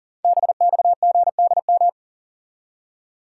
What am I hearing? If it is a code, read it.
Morse code, heard as a keyed tone, about 44 wpm, 700 Hz. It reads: BXGDM
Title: Code Groups - 5 Letters - 40wpm